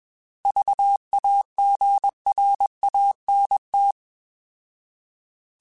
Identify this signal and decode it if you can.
Morse code, heard as a keyed tone, about 21 wpm, 800 Hz. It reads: VAGRANT
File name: Alphabetical / Alphabetical mp3s / Vs @20